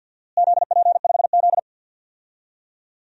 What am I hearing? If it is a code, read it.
Morse code, heard as a keyed tone, about 50 wpm, 700 Hz. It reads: ZP5Z
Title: DXCC-100 Call Signs - 4 Characters - Repeated 2x - 50wpm